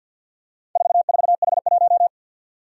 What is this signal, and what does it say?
Morse code, heard as a keyed tone, about 49 wpm, 700 Hz. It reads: 44H1
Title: Four Challenging Letter-Number Combinations 50wpm